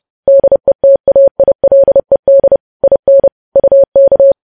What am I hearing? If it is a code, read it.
Morse code, heard as a keyed tone, about 30 wpm, 550 Hz. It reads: DETAILED IN UK
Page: news headlines in morse code at 30 wpm